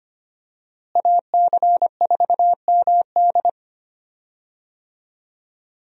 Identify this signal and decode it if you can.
Morse code, heard as a keyed tone, about 25 wpm, 700 Hz. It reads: AC4MD
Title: USA Call Signs - 5 Characters - Repeated 3x - 25wpm